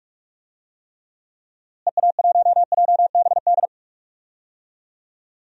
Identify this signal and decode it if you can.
Morse code, heard as a keyed tone, about 41 wpm, 700 Hz. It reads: EA1JBD
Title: DXCC-100 Call Signs - 6 Characters - Repeated 2x - 45wpm